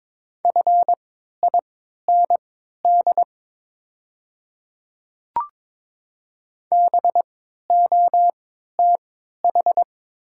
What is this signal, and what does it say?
Morse code, heard as a keyed tone, about 22 wpm, spaced as for 13 wpm, 690 Hz. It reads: FIND BOTH
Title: 3x Character Spacing - Top 500 Words - 22wpm